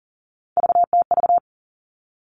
Morse code, heard as a keyed tone, about 40 wpm, 700 Hz. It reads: VTV
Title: All Cut Numbers - 3-Digit 40wpm